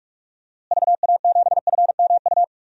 Morse code, heard as a keyed tone, about 46 wpm, 700 Hz. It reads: VA7FMU
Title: DXCC-100 Call Signs - 6 Characters - Repeated 2x - 45wpm